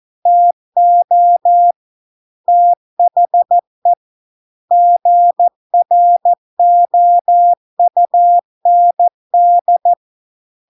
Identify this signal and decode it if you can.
Morse code, heard as a keyed tone, about 14 wpm, 700 Hz. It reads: TO THE GROUND